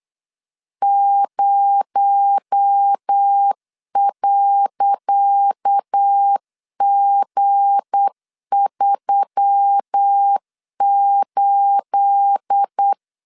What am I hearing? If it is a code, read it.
Morse code, heard as a keyed tone, about 8 wpm, 800 Hz. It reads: 0.G38
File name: LEZIONI / 10 SIGNS / 10C DOT 8wpm 3649